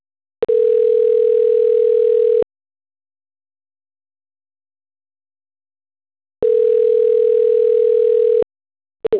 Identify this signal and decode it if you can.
Ringback tone (North America), 440 + 480 Hz.
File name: autotestagent / 11-13-17.136..callcentertransfer. .d052d72a-29e4-4b78-ae75